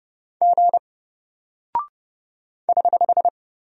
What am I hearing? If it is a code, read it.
Morse code, heard as a keyed tone, about 30 wpm, 700 Hz. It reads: G <HH>
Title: Single Character (Letter:Number:Punctuation:Prosign) Alt 30wpm